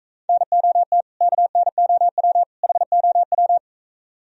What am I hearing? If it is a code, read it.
Morse code, heard as a keyed tone, about 44 wpm, 700 Hz. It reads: NOT KNOW HOW